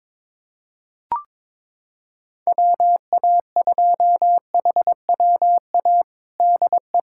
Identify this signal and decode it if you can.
Morse code, heard as a keyed tone, about 22 wpm, 700 Hz. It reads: WA2HWA DE
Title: USA Callsign DE Callsign - 6 Characters - 22wpm